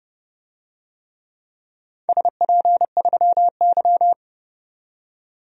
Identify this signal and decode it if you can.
Morse code, heard as a keyed tone, about 29 wpm, 700 Hz. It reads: SP3Y
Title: DXCC-100 Call Signs - 4 Characters - 30wpm